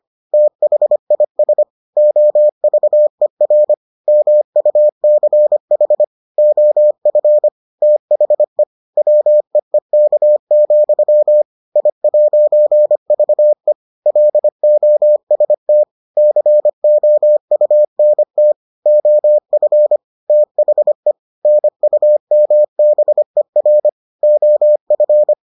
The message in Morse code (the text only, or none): THIS OVER MUCH OF THE WEEK, I'VE LOST COUNT OF THE NUMBER OF